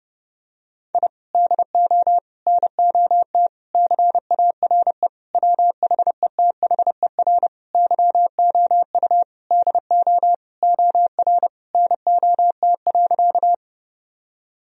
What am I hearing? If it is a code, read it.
Morse code, heard as a keyed tone, about 29 wpm, 700 Hz. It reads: I DO NOT CARE WHETHER YOU DO OR NOT.
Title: Sentences from Top 1000 Words - Review Part 3 - 30wpm